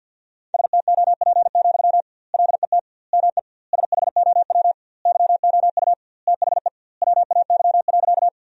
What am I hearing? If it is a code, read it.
Morse code, heard as a keyed tone, about 50 wpm, 700 Hz. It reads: STOP, LET ME SHOW YOU THE WAY.
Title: 2x Word-Spacing - Sentences from Top 300 Words - 50wpm